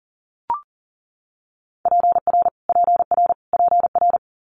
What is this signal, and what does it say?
Morse code, heard as a keyed tone, about 39 wpm, 700 Hz. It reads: PR PR PR
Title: Multi-letter Phonemes Repeated 3x 40wpm